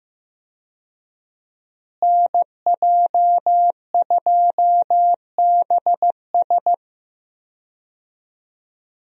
Morse code, heard as a keyed tone, about 15 wpm, 700 Hz. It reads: NJ2BS
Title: USA Call Signs - 5 Characters - Repeated 3x - 15wpm